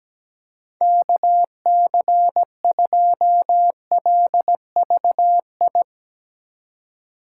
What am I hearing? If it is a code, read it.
Morse code, heard as a keyed tone, about 17 wpm, 700 Hz. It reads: KC2LVI
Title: USA Call Signs - 6 Characters - 3x Character Spacing - 17wpm